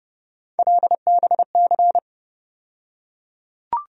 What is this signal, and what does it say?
Morse code, heard as a keyed tone, about 30 wpm, 700 Hz. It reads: LBC
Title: USA Call Sign Postfixes - 3 Letters 30wpm